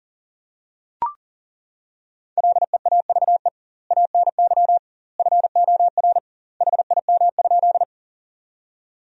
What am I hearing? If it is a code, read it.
Morse code, heard as a keyed tone, about 37 wpm, 700 Hz. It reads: LEAVE ANY FOR HIM?
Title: 2x Word-Spacing - Sentences from Top 400 Words - 40wpm